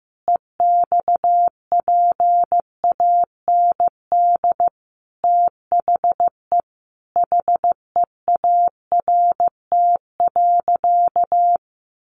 Morse code, heard as a keyed tone, about 15 wpm, 700 Hz. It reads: EXPAND THE HEART.
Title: Sentences from Top 1400 Words 15wpm